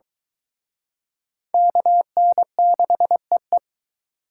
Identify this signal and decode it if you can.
Morse code, heard as a keyed tone, about 23 wpm, 700 Hz. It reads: KN6EE